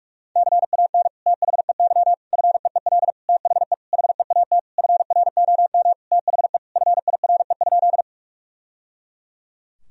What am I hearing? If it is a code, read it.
Morse code, heard as a keyed tone, about 46 wpm, 700 Hz. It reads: CAN THEY FEEL THE HEAT FROM THE FIRE?
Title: Sentences from Top 1000 Words - Review Part 3 - 45wpm